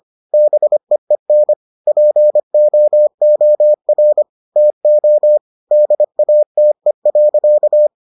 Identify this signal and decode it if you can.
Morse code, heard as a keyed tone, about 25 wpm, 600 Hz. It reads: BEEN POOR TO DATE.